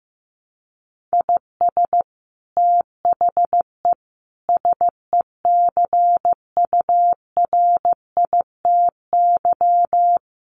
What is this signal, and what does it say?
Morse code, heard as a keyed tone, about 15 wpm, 700 Hz. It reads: IS THE SECURITY
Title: Sentences from Top 1400 Words 15wpm